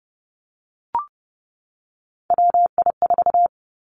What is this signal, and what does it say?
Morse code, heard as a keyed tone, about 30 wpm, 700 Hz. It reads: WI4
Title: Three Challenging Letter-Number Combinations 30wpm